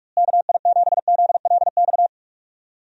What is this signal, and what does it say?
Morse code, heard as a keyed tone, about 46 wpm, 700 Hz. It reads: KI7ZLX
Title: USA Call Signs - 6 Characters - Repeated 2x - 45wpm